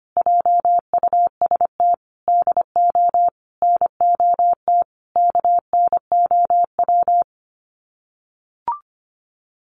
Morse code, heard as a keyed tone, about 25 wpm, 690 Hz. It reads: JUST DO NOT KNOW
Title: Sets of 5 Words Encore 25wpm